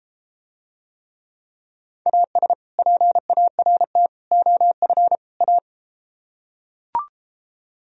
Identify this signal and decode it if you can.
Morse code, heard as a keyed tone, about 33 wpm, 700 Hz. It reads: AS PART OF A